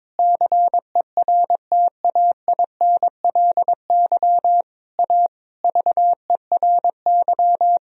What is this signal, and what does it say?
Morse code, heard as a keyed tone, about 22 wpm, 700 Hz. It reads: CERTAINLY A VERY
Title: Sentences from Top 1000 Words - Review Part 3 - 22wpm